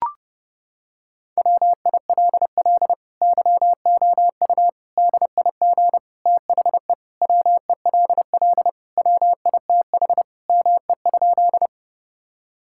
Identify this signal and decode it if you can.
Morse code, heard as a keyed tone, about 30 wpm, 700 Hz. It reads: WILL YOU DIG THE WELL WITH ME?
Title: Sentences from Top 1600 Words 30wpm